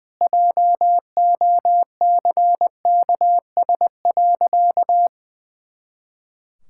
Morse code, heard as a keyed tone, about 20 wpm, 700 Hz. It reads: JOCKS.
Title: Pangram Challenge v2 - 20wpm